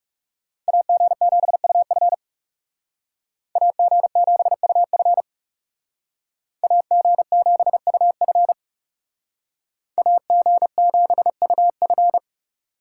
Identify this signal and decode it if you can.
Morse code, heard as a keyed tone, about 35 wpm, 700 Hz. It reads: AG7UF AG7UF AG7UF AG7UF
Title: Speed Racing - USA 2x2 Call Signs 30wpm